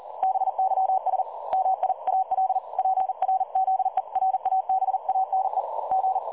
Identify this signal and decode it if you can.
Morse code, heard as a keyed tone, about 40 wpm, 760 Hz. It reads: BYS MIAL PRZERABANE G
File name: dlazaawansowanych / nagraniazpasma / websdr recording 3565.0kHz